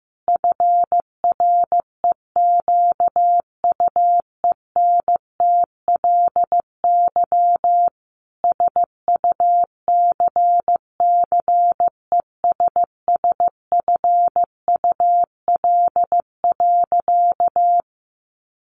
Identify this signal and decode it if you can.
Morse code, heard as a keyed tone, about 14 wpm, 700 Hz. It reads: FREQUENTLY SUCCESSFUL.
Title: Sentences from Top 1300 Words 15wpm